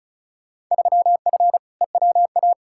Morse code, heard as a keyed tone, about 34 wpm, 700 Hz. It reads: 3F EWA
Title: Field Day Exchanges Encore - 35wpm